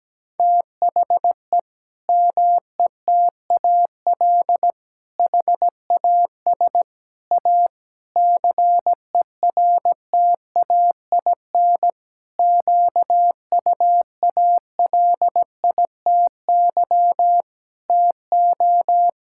Morse code, heard as a keyed tone, about 17 wpm, 700 Hz. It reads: THE METAL HAS A CERTAIN QUALITY TO
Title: Sentences from Top 1000 Words - Review Part 6 - 17wpm